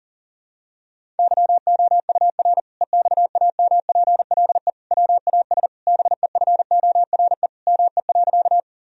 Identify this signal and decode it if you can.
Morse code, heard as a keyed tone, about 40 wpm, 700 Hz. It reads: YOUR EXAMPLE WAS BEFORE ME.
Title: Sentences from Top 1000 Words - Review Part 1 - 40wpm